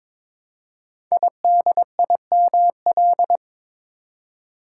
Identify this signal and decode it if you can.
Morse code, heard as a keyed tone, about 22 wpm, 700 Hz. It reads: IDIML
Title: Code Groups - 5 Letter:Numbers - 22wpm